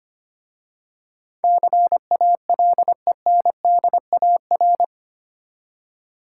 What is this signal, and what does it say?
Morse code, heard as a keyed tone, about 25 wpm, 700 Hz. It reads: CALENDAR
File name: practice / 25wpm / Easy Words with Hints - 25wpm